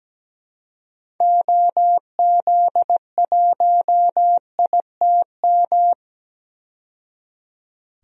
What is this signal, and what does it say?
Morse code, heard as a keyed tone, about 17 wpm, 700 Hz. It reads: OZ1ITM
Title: DXCC-100 Call Signs - 6 Characters - 17wpm